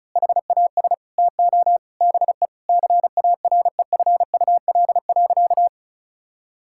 Morse code, heard as a keyed tone, about 36 wpm, 700 Hz. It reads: HAS TO BE CAREFUL.
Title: Sentences from Top 1600 Words 35wpm